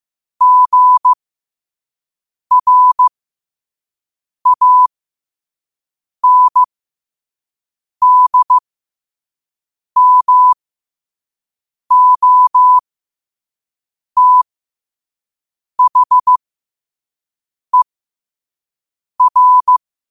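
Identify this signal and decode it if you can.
Morse code, heard as a keyed tone, about 15 wpm, spaced as for 5 wpm, 1000 Hz. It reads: GRANDMOTHER
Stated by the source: Morse code practice at 5 wpm.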